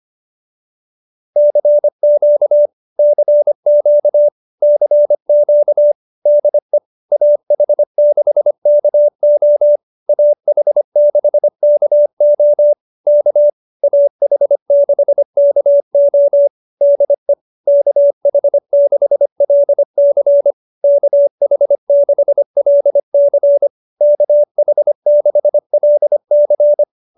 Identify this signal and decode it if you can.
Morse code, heard as a keyed tone, about 25 wpm, 590 Hz. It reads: CQ CQ CQ DE AH6KO AH6KO K AH6KO DE KH6LC KH6LC KH6LC